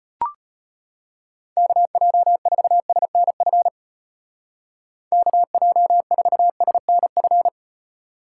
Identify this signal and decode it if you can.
Morse code, heard as a keyed tone, about 30 wpm, 700 Hz. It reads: KJ4SNF KJ4SNF
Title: Speed Racing - USA 2x3 Call Signs 25wpm